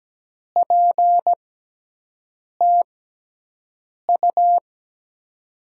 Morse code, heard as a keyed tone, about 17 wpm, spaced as for 6 wpm, 700 Hz. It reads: PTU
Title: USA Call Signs - 6 Characters - 6x Character Spacing - 17wpm